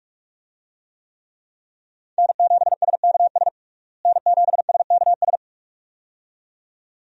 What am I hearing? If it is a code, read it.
Morse code, heard as a keyed tone, about 46 wpm, 700 Hz. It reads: N7SKS N7SKS
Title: USA Call Signs - 5 Characters - Repeated 2x - 45wpm